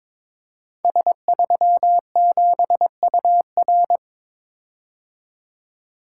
Morse code, heard as a keyed tone, about 22 wpm, 700 Hz. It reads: S37UR
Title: Code Groups - 5 Letter:Numbers - 22wpm